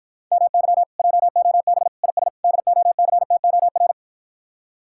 Morse code, heard as a keyed tone, about 52 wpm, 700 Hz. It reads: MY JOB IS DOCTOR